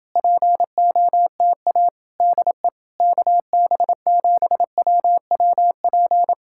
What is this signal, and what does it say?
Morse code, heard as a keyed tone, about 29 wpm, 700 Hz. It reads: POTA DE KB7WWP